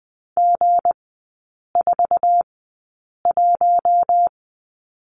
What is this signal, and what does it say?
Morse code, heard as a keyed tone, about 20 wpm, spaced as for 8 wpm, 690 Hz. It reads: G41
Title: 159 Lesson 26e - Letter g - Three Characters 20wpm